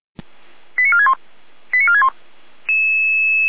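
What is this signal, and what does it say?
Five-tone selective call (ZVEI), heard as quick runs of five tones, tones of about 70 ms. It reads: 89461 89461
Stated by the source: ZVEI five-tone selective call received on an SDR.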